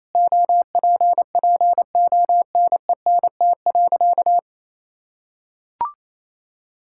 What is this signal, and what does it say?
Morse code, heard as a keyed tone, about 28 wpm, 700 Hz. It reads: OPPONENT.
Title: Pangram Challenge v2 - 28wpm